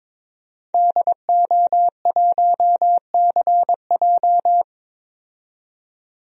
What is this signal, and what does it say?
Morse code, heard as a keyed tone, about 21 wpm, 700 Hz. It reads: DO1CJ
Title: DXCC-100 Call Signs - 5 Characters - 22wpm